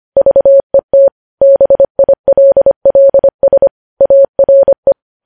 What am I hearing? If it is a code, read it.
Morse code, heard as a keyed tone, about 25 wpm, 550 Hz. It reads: VET BILLS ARE
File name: podcast / News headlines-1x-25x25